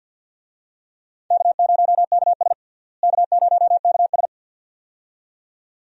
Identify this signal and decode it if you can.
Morse code, heard as a keyed tone, about 52 wpm, 700 Hz. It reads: K0KS K0KS
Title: USA Call Signs - 4 Characters - Repeated 2x - 50wpm